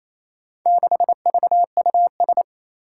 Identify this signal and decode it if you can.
Morse code, heard as a keyed tone, about 28 wpm, 700 Hz. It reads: 6VUS
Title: Four Challenging Letter-Number Combinations 28wpm